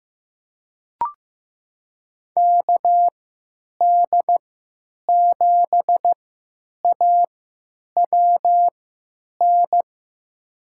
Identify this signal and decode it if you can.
Morse code, heard as a keyed tone, about 15 wpm, spaced as for 9 wpm, 700 Hz. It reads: KD7AWN
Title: USA Call Signs - 6 Characters - 3x Character Spacing - 15wpm